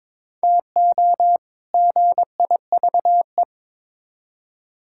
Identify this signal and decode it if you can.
Morse code, heard as a keyed tone, about 22 wpm, 700 Hz. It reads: TO GIVE
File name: Build A Phrase - 2-Word Phrases - 22wpm